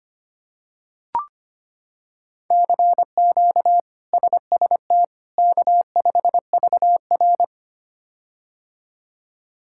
Morse code, heard as a keyed tone, about 25 wpm, 700 Hz. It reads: CQ SST K5VR
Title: K1USN SST QSO Practice - 25wpm